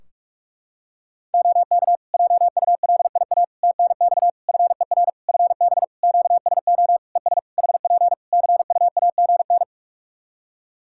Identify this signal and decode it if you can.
Morse code, heard as a keyed tone, about 41 wpm, 700 Hz. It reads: OK JULIA TNX FER FB QSO ES HP CUAGN
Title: Practice QSO - Exchange 4 - Ending - 45wpm